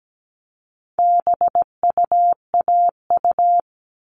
Morse code, heard as a keyed tone, about 17 wpm, 700 Hz. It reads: BUAU
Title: All Cut Numbers - 4-Digit 17wpm